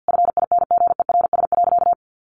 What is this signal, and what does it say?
Morse code, heard as a keyed tone, about 48 wpm, 700 Hz. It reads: FINGERS.
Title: A Sound Of Thunder 50wpm